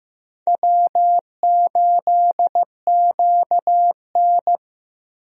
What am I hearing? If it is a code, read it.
Morse code, heard as a keyed tone, about 15 wpm, 700 Hz. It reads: W8QN
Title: USA Call Signs - 4 Characters - Repeated 3x - 15wpm